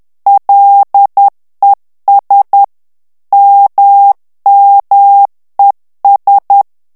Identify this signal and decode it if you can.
Morse code, heard as a keyed tone, about 10 wpm, 800 Hz. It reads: LES MMES
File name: CW / FICHIERS-MP3 / FRANCAIS TEXTE CLAIR 2 VITESSE 10 MOTS MINUTES